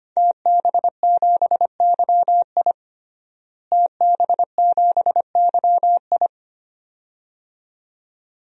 Morse code, heard as a keyed tone, about 25 wpm, 700 Hz. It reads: TB7YI TB7YI
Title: DXCC-100 Call Signs - 5 Characters - Repeated 2x - 25wpm